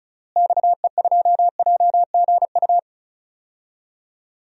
Morse code, heard as a keyed tone, about 34 wpm, 700 Hz. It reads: XE2JGU
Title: DXCC-100 Call Signs - 6 Characters - Repeated 2x - 35wpm